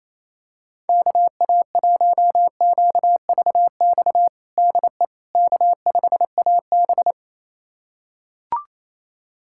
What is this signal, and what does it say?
Morse code, heard as a keyed tone, about 28 wpm, 700 Hz. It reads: KA1QVX DE K5AB
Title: USA Callsign DE Callsign 28wpm